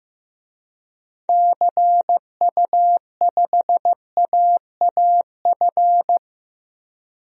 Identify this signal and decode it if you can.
Morse code, heard as a keyed tone, about 15 wpm, 700 Hz. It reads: CU5AAF
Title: DXCC-100 Call Signs - 6 Characters - Repeated 3x - 15wpm